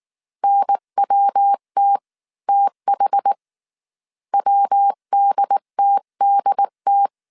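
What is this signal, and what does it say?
Morse code, heard as a keyed tone, about 19 wpm, spaced as for 17 wpm, 800 Hz. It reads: NWTTH WDTDT
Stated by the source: Morse code dictation at 18 wpm from the IW3SGT QRS Morse code training course, five-character groups of 300 characters.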